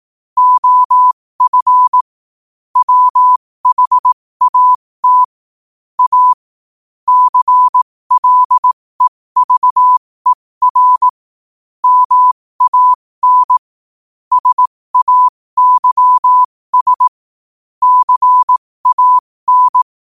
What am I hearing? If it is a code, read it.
Morse code, heard as a keyed tone, about 18 wpm, spaced as for 15 wpm, 1000 Hz. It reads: OF WHAT A CLEVER MAN SAYS CAN
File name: qotdfiles / 15wpm